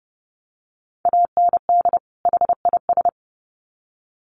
Morse code, heard as a keyed tone, about 30 wpm, 700 Hz. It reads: AND HIS
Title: Sets of 2 Words 30wpm